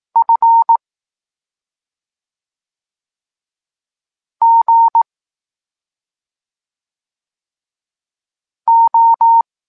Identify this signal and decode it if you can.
Morse code, heard as a keyed tone, about 18 wpm, 910 Hz. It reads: F G O